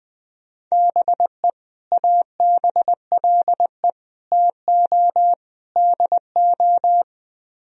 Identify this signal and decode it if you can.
Morse code, heard as a keyed tone, about 20 wpm, 700 Hz. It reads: BE ABLE TO DO
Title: Speed-Racing - Sets of 4 Words 20wpm